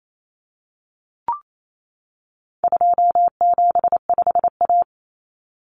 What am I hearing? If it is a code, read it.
Morse code, heard as a keyed tone, about 28 wpm, 700 Hz. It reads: 275A
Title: Common Cut Numbers - 4-Digit 28wpm